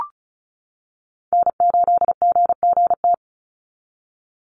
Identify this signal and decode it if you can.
Morse code, heard as a keyed tone, about 34 wpm, 700 Hz. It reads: N8GGT
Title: USA Call Signs 35wpm